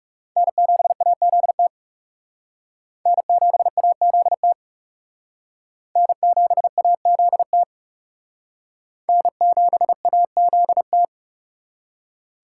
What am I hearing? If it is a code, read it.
Morse code, heard as a keyed tone, about 36 wpm, 700 Hz. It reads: N7AZT N7AZT N7AZT N7AZT
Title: Speed Racing - USA 1x3 Call Signs 30wpm